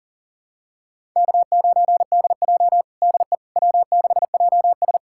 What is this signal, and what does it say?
Morse code, heard as a keyed tone, about 40 wpm, 700 Hz. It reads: K9DJ DE W6JS
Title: USA Callsign DE Callsign - 4 Characters - 40wpm